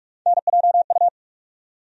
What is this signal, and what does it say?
Morse code, heard as a keyed tone, about 44 wpm, 700 Hz. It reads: NJU
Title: USA Call Sign Postfixes - 3 Letters 45wpm